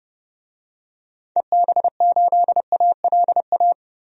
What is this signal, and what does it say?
Morse code, heard as a keyed tone, about 32 wpm, 700 Hz. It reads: EB8ALA
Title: DXCC-100 Call Signs - 6 Characters - 30wpm